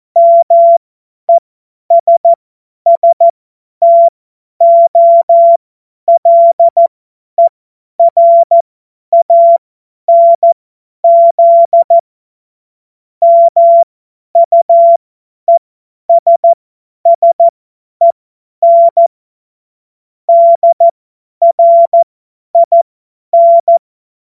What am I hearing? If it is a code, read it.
Morse code, heard as a keyed tone, about 14 wpm, spaced as for 10 wpm, 680 Hz. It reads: MESSTOLERANZ MUESSEN DRIN
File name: Morsekurs / Witze / Witz 26. April 2022 50 F70